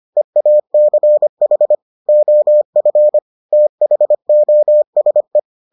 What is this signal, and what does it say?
Morse code, heard as a keyed tone, about 25 wpm, 600 Hz. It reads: EACH OF THOSE